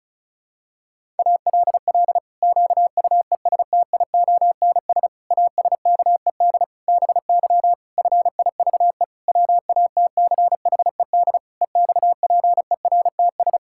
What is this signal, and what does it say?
Morse code, heard as a keyed tone, about 36 wpm, 700 Hz. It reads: ALL QUESTIONS ASKED BY FIVE WATCHED EXPERTS
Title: Pangram Challenge v2 - 35wpm